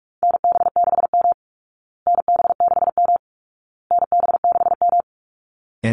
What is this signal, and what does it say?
Morse code, heard as a keyed tone, about 44 wpm, 700 Hz. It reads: NB6M NB6M NB6M
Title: Call Signs Repeated 2x - 45wpm